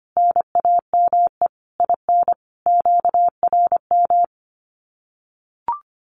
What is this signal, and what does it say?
Morse code, heard as a keyed tone, about 25 wpm, 700 Hz. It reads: NAME IN QRM
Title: Common QSO Phrases 25wpm